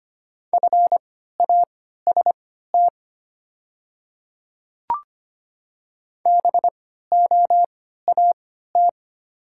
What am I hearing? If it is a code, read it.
Morse code, heard as a keyed tone, about 25 wpm, spaced as for 14 wpm, 700 Hz. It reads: FAST BOAT
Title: 3x Character Spacing - Top 500 Words - 25wpm